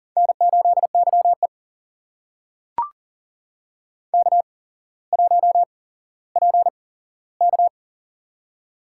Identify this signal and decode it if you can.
Morse code, heard as a keyed tone, about 41 wpm, spaced as for 17 wpm, 700 Hz. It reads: N8YE K1PK
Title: USA Call Signs - 4 Characters - 8x Character Spacing - 40wpm